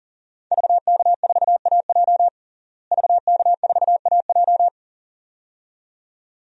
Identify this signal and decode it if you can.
Morse code, heard as a keyed tone, about 40 wpm, 690 Hz. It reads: VK4AJ VK4AJ
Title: DXCC-100 Call Signs - 5 Characters - Repeated 2x - 40wpm